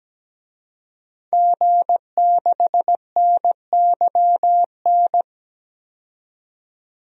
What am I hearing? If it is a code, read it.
Morse code, heard as a keyed tone, about 17 wpm, 700 Hz. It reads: G6NYN
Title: DXCC-100 Call Signs - 5 Characters - 17wpm